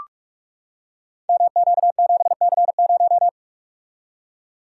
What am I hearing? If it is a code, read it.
Morse code, heard as a keyed tone, about 44 wpm, 690 Hz. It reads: MQ7C0
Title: Code Groups - 5 Letter:Numbers - 45wpm